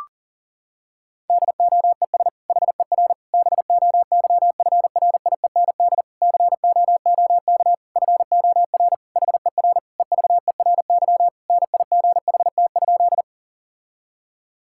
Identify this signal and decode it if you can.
Morse code, heard as a keyed tone, about 40 wpm, 700 Hz. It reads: DOES HER BOYFRIEND COOK FOR HER EVERY NIGHT?
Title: Sentences from Top 1000 Words - Review Part 5 - 40wpm